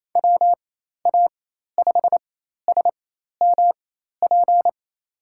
Morse code, heard as a keyed tone, about 28 wpm, spaced as for 13 wpm, 700 Hz. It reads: WA5SMP
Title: USA Call Signs - 6 Characters - 4x Character Spacing - 28wpm